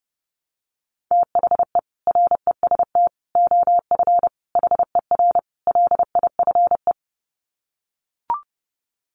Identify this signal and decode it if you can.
Morse code, heard as a keyed tone, about 30 wpm, 700 Hz. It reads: THE REST OF HER LIFE
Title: Sets of 5 Words Encore 30wpm